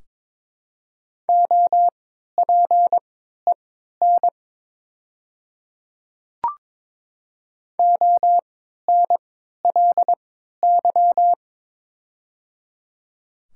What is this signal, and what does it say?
Morse code, heard as a keyed tone, about 22 wpm, spaced as for 13 wpm, 700 Hz. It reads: OPEN ONLY
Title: 3x Character Spacing - Top 300 Words - 22wpm